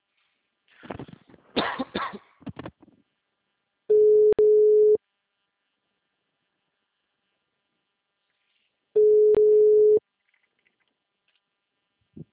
Ringback tone (Europe, 425 Hz).